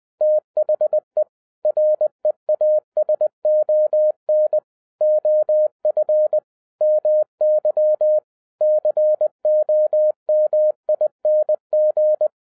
Morse code, heard as a keyed tone, about 20 wpm, 610 Hz. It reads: THE REASON OF MY COMING
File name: 2012-01-mp3 / n05-CAME BACK TO HAM RADIO AFTER 30 YEARS OF INTERVAL-20wpm